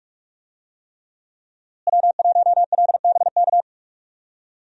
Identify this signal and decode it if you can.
Morse code, heard as a keyed tone, about 46 wpm, 700 Hz. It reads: W1LBK
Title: USA Call Signs - 5 Characters - Repeated 3x - 45wpm